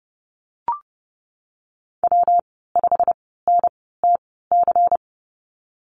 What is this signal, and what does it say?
Morse code, heard as a keyed tone, about 30 wpm, spaced as for 17 wpm, 700 Hz. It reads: W5NTC
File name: Lesson 19g - Letter w - Call Signs 30wpm